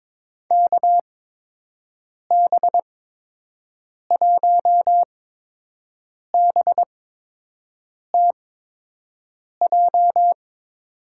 Morse code, heard as a keyed tone, about 22 wpm, spaced as for 6 wpm, 700 Hz. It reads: KB1BTJ
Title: USA Call Signs - 6 Characters - 8x Character Spacing - 22wpm